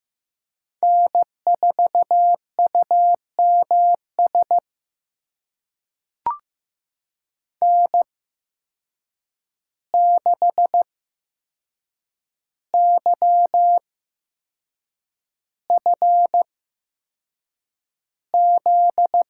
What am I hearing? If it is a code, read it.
Morse code, heard as a keyed tone, about 15 wpm, spaced as for 6 wpm, 700 Hz. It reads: N4UMS N6YFZ
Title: USA Call Signs - 8x Character Spacing - 15wpm